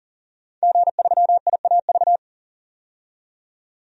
Morse code, heard as a keyed tone, about 40 wpm, 700 Hz. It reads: G3IAV
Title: DXCC-100 Call Signs - 5 Characters - 40wpm